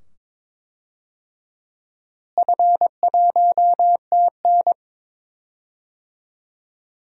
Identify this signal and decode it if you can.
Morse code, heard as a keyed tone, about 22 wpm, 700 Hz. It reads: F1TN
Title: DXCC-100 Call Signs - 4 Characters - Repeated 2x - 22wpm